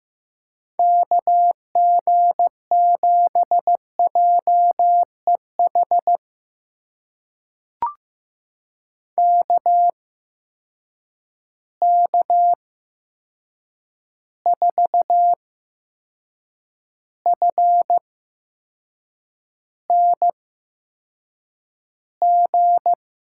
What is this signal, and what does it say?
Morse code, heard as a keyed tone, about 15 wpm, spaced as for 6 wpm, 700 Hz. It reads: KG7JEH KK4FNG
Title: USA Call Signs - 6 Characters - 8x Character Spacing - 15wpm